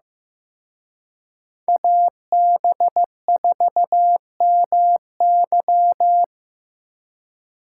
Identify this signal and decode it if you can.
Morse code, heard as a keyed tone, about 18 wpm, 700 Hz. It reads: AB4MY
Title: Speed Racing - USA 2x2 Call Signs 15wpm